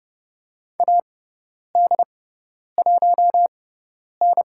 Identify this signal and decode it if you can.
Morse code, heard as a keyed tone, about 30 wpm, spaced as for 10 wpm, 700 Hz. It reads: AD1N
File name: Lesson 16g - Letter c - Call Signs 30wpm